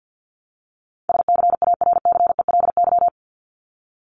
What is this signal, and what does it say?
Morse code, heard as a keyed tone, about 50 wpm, 700 Hz. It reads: SCARCELY
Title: LY Suffix - Top 100 Words with Base Words 50wpm